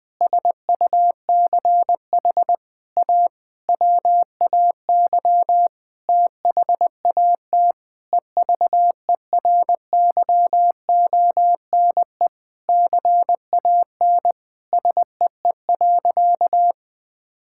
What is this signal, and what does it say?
Morse code, heard as a keyed tone, about 21 wpm, 700 Hz. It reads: SUCH A WAY THAT EVERYONE CAN SEE.
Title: Sentences with 5-Word Phrases - 20wpm